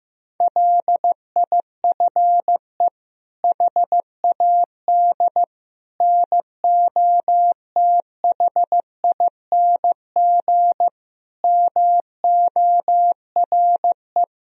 Morse code, heard as a keyed tone, about 16 wpm, 700 Hz. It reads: LIFE HAD NOTHING MORE
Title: Sentences from Top 1000 Words - Review Part 4 - 15wpm